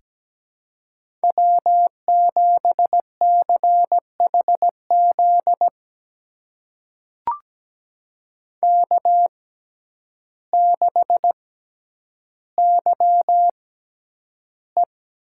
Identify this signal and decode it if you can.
Morse code, heard as a keyed tone, about 17 wpm, spaced as for 9 wpm, 700 Hz. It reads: W7CHZ K6YE
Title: USA Call Signs - 5 Characters - 6x Character Spacing - 17wpm